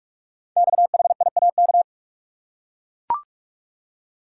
Morse code, heard as a keyed tone, about 44 wpm, 700 Hz. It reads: XHIAK
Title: Code Groups - 5 Letter:Numbers - 45wpm